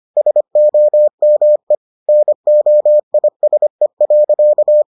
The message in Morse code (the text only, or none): SOME NOISE.